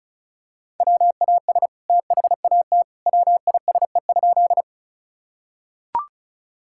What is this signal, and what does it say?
Morse code, heard as a keyed tone, about 34 wpm, 700 Hz. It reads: WAS THAT WISE?
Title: Sentences from Top 1600 Words 35wpm